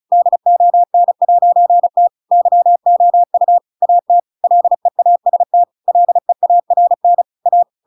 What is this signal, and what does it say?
Morse code, heard as a keyed tone, about 34 wpm, 700 Hz. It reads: DON'T YOU AT LEAST LEARN A